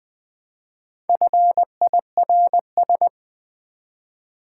Morse code, heard as a keyed tone, about 20 wpm, 700 Hz. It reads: FIRS
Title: Build A Word - 1 Letter At A Time - 5-Letter Words - 20wpm